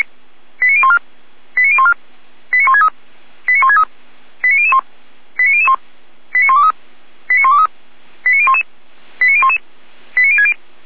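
Five-tone selective call (ZVEI), heard as quick runs of five tones, tones of about 70 ms. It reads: 89014 89014 89163 89163 89001 89001 89123 89123 89011 89011 89060
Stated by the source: ZVEI five-tone call 89130 received by SDR on channel 455, fire brigade Orlamuende (feuerwehr-orlamuende).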